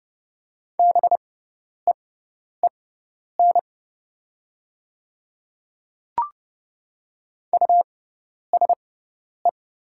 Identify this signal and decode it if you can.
Morse code, heard as a keyed tone, about 30 wpm, spaced as for 10 wpm, 700 Hz. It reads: BEEN USE
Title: 6x Character Spacing - Top 100 Words - 30wpm